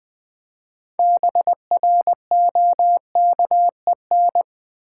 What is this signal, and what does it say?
Morse code, heard as a keyed tone, about 20 wpm, 700 Hz. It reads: BROKEN
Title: 6 Letter Words - First 3 Letters Given - 20wpm